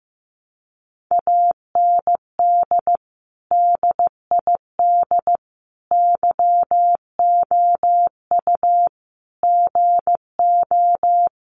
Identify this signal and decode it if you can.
Morse code, heard as a keyed tone, about 15 wpm, 700 Hz. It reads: AND DID YOU GO
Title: Sentences from Top 1000 Words 15wpm